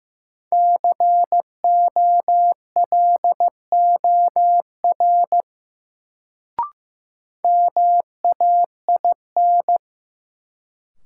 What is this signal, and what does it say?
Morse code, heard as a keyed tone, about 15 wpm, 700 Hz. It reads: COLOR MAIN
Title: Sentences from Top 1000 Words - Review Part 2 - New Word Drill - 15wpm